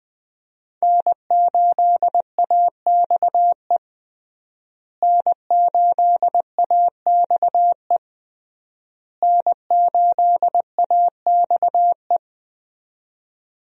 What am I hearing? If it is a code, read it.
Morse code, heard as a keyed tone, about 21 wpm, 700 Hz. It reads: N8AXE N8AXE N8AXE
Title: USA Call Signs - 5 Characters - Repeated 3x - 20wpm